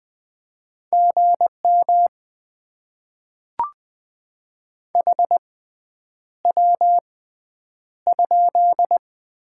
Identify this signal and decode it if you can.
Morse code, heard as a keyed tone, about 20 wpm, 700 Hz. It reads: G M H W ?
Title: Top 20 QSO Elements - 6x Character Spacing - 20wpm